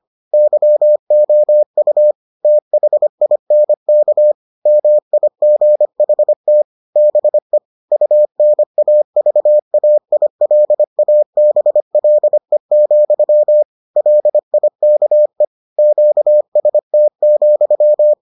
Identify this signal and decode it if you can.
Morse code, heard as a keyed tone, about 25 wpm, 600 Hz. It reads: YOU THINK MIGHT BE UNAVAILABLE, LIKE QST,